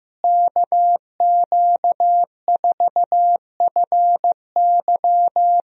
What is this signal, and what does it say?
Morse code, heard as a keyed tone, about 15 wpm, 700 Hz. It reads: KQ4FY
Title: USA Call Signs - 5 Characters - Repeated 2x - 15wpm